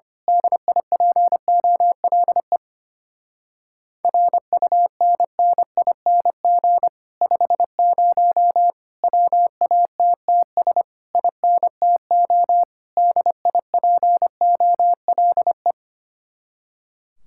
Morse code, heard as a keyed tone, about 24 wpm, 700 Hz. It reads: DIPOLE RUNNING 50 WATTS INTO DIPOLE
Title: Speed-Racing Common QSO Phrases Encore - 25wpm